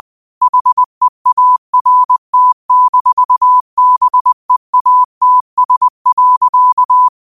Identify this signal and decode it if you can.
Morse code, heard as a keyed tone, about 20 wpm, 1000 Hz. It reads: HEART-BEATS.